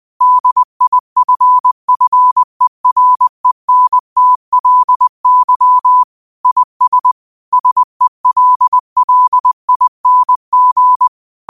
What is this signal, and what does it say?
Morse code, heard as a keyed tone, about 20 wpm, 1000 Hz. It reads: DIFFERENTLY IS SELLING
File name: qotdfiles / 20wpm